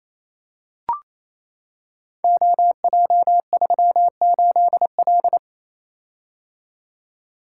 Morse code, heard as a keyed tone, about 27 wpm, 700 Hz. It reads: OJ38L
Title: Code Groups - 5 Letter:Numbers - 28wpm